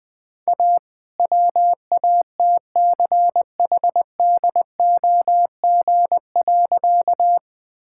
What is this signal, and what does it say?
Morse code, heard as a keyed tone, about 20 wpm, 700 Hz. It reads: A WATCHDOG.
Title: All-Age Appropriate Jokes - 20wpm